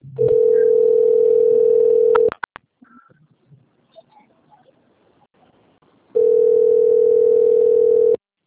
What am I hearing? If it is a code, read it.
Ringback tone (North America), 440 + 480 Hz.
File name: rg-1600-unknown- - - .34557